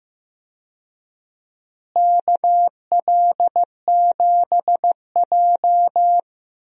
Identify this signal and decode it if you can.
Morse code, heard as a keyed tone, about 15 wpm, 700 Hz. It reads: KL7J
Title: DXCC-100 Call Signs - 4 Characters - Repeated 3x - 15wpm